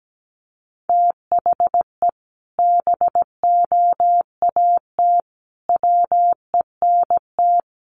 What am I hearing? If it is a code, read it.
Morse code, heard as a keyed tone, about 17 wpm, 700 Hz. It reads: THE BOAT WENT
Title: Sentences from Top 1100 Words 17wpm